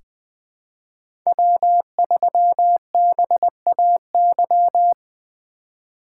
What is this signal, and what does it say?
Morse code, heard as a keyed tone, about 20 wpm, 700 Hz. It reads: W3BAY
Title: 2023 POTA Activator Call Signs - 20wpm